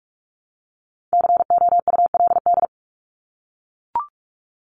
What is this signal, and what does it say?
Morse code, heard as a keyed tone, about 44 wpm, 700 Hz. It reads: COULD
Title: Top 200 Words Repeated 45wpm